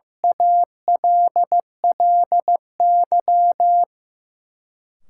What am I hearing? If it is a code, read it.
Morse code, heard as a keyed tone, about 15 wpm, 700 Hz. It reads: ALLY
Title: Sentences from Top 1400 Words - New Word Drill Alt - 15wpm